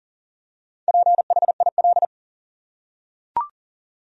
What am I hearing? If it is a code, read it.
Morse code, heard as a keyed tone, about 40 wpm, 700 Hz. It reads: PHIL
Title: USA Common Names - Max 5 Letters - 40wpm